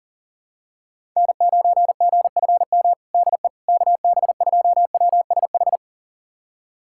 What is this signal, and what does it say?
Morse code, heard as a keyed tone, about 40 wpm, 700 Hz. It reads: N9GFM DE KB2WSH
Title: USA Callsign DE Callsign 40wpm